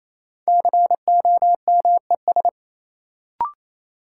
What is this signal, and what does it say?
Morse code, heard as a keyed tone, about 28 wpm, 700 Hz. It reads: COMES
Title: Sentences from Top 1000 Words - Review Part 5 - New Word Drill - 28wpm